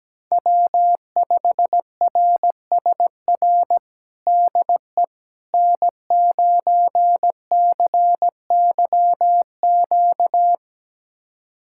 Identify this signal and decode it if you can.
Morse code, heard as a keyed tone, about 17 wpm, 700 Hz. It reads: W5RSR DE N9CYQ
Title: USA Callsign DE Callsign - 5 Characters - 17wpm